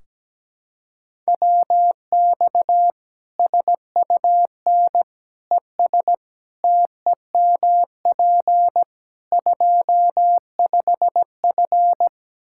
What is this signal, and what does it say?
Morse code, heard as a keyed tone, about 17 wpm, 700 Hz. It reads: WX SUN ES TEMP 25F
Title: Practice QSO - Exchange 2 - Info - 17wpm